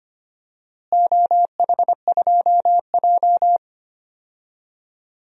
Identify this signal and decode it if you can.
Morse code, heard as a keyed tone, about 25 wpm, 700 Hz. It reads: OH2J
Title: DXCC-100 Call Signs - 4 Characters - Repeated 3x - 25wpm